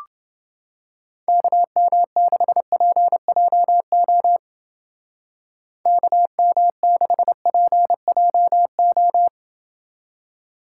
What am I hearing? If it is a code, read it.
Morse code, heard as a keyed tone, about 29 wpm, 700 Hz. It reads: KM6PJO KM6PJO